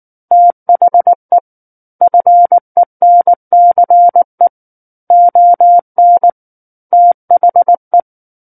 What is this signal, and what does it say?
Morse code, heard as a keyed tone, about 19 wpm, 700 Hz. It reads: THE FENCE ON THE
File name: Book One / 10-Chapter 10 - 19 WPM